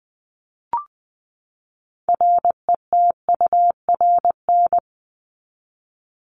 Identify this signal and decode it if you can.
Morse code, heard as a keyed tone, about 20 wpm, 700 Hz. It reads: RETURN
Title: ED Suffix - Top 100 Words with Base Words 20wpm